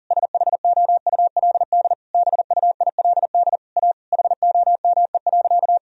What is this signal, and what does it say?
Morse code, heard as a keyed tone, about 40 wpm, 700 Hz. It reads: SHOULD BUILD A HOME.
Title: Sentences from Top 1000 Words - Review Part 1 - 40wpm